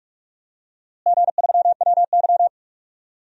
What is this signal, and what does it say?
Morse code, heard as a keyed tone, about 46 wpm, 700 Hz. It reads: G3WY
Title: DXCC-100 Call Signs - 4 Characters - Repeated 3x - 45wpm